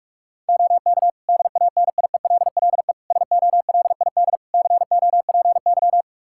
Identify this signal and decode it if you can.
Morse code, heard as a keyed tone, about 45 wpm, 700 Hz. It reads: OK DANIELLE SOLID COPY
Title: Practice QSO - Exchange 3 - More Info - 45wpm